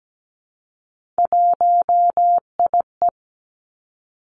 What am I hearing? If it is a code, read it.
Morse code, heard as a keyed tone, about 17 wpm, 700 Hz. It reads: 1IE
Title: Three Challenging Letter-Number Combinations 17wpm